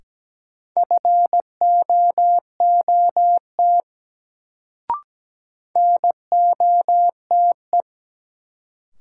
Morse code, heard as a keyed tone, about 17 wpm, 700 Hz. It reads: FOOT NOTE
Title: Sets of 5 Words - New Word Drill - 17wpm